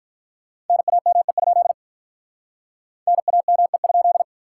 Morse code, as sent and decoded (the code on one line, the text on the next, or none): -. .- -- . ..--.. / -. .- -- . ..--..
NAME? NAME?